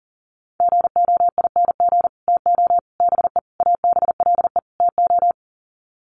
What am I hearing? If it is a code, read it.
Morse code, heard as a keyed tone, about 40 wpm, 700 Hz. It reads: GOING TO BE ABLE TO
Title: Sets of 5 Words 40wpm